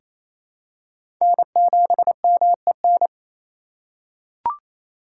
Morse code, heard as a keyed tone, about 28 wpm, 700 Hz. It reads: N7MEN
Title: USA Call Signs - 5 Characters - 4x Character Spacing - 28wpm